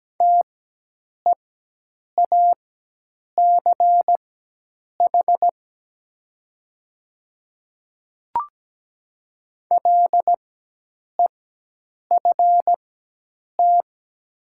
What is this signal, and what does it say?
Morse code, heard as a keyed tone, about 17 wpm, spaced as for 8 wpm, 700 Hz. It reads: TEACH LEFT
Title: 4x Character Spacing - Top 500 Words - 17wpm